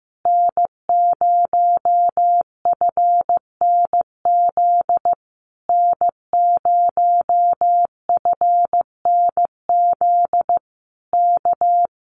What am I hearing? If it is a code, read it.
Morse code, heard as a keyed tone, about 15 wpm, 700 Hz. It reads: N0FNZ N0FNZ K
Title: Common QSO Phrases Encore 15wpm